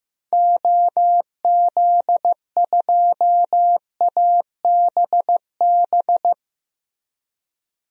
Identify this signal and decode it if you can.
Morse code, heard as a keyed tone, about 16 wpm, 700 Hz. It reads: OZ2ABB
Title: DXCC-100 Call Signs - 6 Characters - Repeated 3x - 15wpm